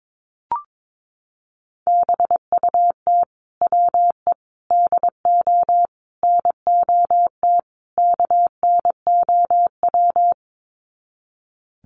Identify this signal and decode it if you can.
Morse code, heard as a keyed tone, about 22 wpm, 700 Hz. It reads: BUT WE DO NOT KNOW
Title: Sets of 5 Words Encore2 22wpm